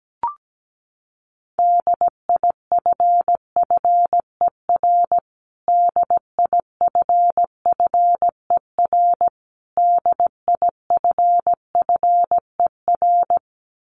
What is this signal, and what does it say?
Morse code, heard as a keyed tone, about 17 wpm, 700 Hz. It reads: DIFFER DIFFER DIFFER
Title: Top 500 Words Repeated 17wpm